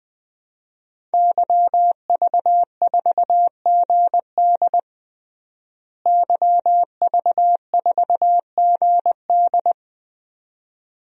Morse code, heard as a keyed tone, about 21 wpm, 700 Hz. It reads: YV4GD YV4GD
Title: DXCC-100 Call Signs - 5 Characters - Repeated 2x - 20wpm